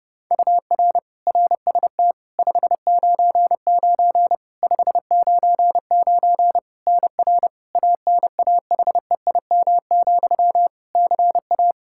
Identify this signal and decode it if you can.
Morse code, heard as a keyed tone, about 30 wpm, 700 Hz. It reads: UR RST 599 599 NR ANAHEIM, CA
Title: Practice QSO - Exchange 1 - Beginning - 30wpm